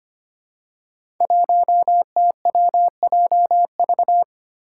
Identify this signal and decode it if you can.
Morse code, heard as a keyed tone, about 25 wpm, 700 Hz. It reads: 1TWJV
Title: Five Challenging Letter-Number Combinations 25wpm